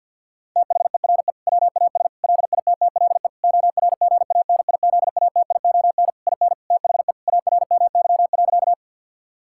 Morse code, heard as a keyed tone, about 50 wpm, 690 Hz. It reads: THERE WAS LITTLE ORGANIZATION IN THE ARMY.
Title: Sentences from Top 1000 Words - Review Part 5 - 50wpm